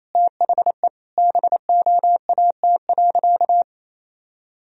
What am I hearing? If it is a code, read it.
Morse code, heard as a keyed tone, about 28 wpm, 700 Hz. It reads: THE BOAT.
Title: Sentences from Top 1000 Words - Review Part 3 - 28wpm